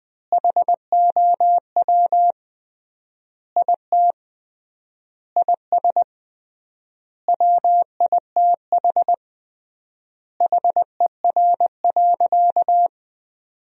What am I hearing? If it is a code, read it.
Morse code, heard as a keyed tone, about 20 wpm, 700 Hz. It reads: HOW IT IS WITH HER.
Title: 3x Word-Spacing - Sentences from Top 100 Words - 20wpm